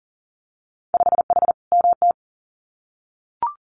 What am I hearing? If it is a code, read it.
Morse code, heard as a keyed tone, about 40 wpm, 700 Hz. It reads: 5H MT
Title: Winter Field Day Exchanges Encore 40wpm